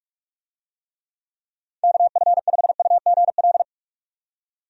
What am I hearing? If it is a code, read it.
Morse code, heard as a keyed tone, about 46 wpm, 700 Hz. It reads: KF5UGL
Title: USA Call Signs - 6 Characters - 45wpm